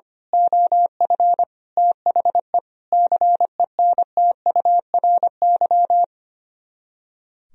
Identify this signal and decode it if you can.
Morse code, heard as a keyed tone, about 24 wpm, 700 Hz. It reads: OF THE CENTURY
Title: Speed-Racing - Sets of 5 Words 25wpm